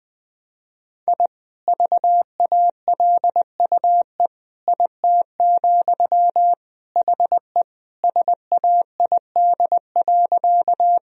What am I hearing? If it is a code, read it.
Morse code, heard as a keyed tone, about 20 wpm, 700 Hz. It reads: I VALUE IT, HE SAID.
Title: Sentences from Top 1000 Words - Review Part 3 - 20wpm